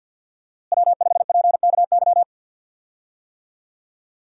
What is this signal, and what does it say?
Morse code, heard as a keyed tone, about 46 wpm, 700 Hz. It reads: W5PKY
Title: USA Call Signs - 5 Characters - Repeated 3x - 50wpm